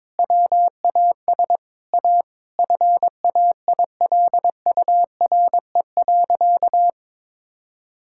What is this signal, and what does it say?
Morse code, heard as a keyed tone, about 22 wpm, 700 Hz. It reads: WAS A FAILURE.
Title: Sentences from Top 1000 Words - Review Part 6 - 22wpm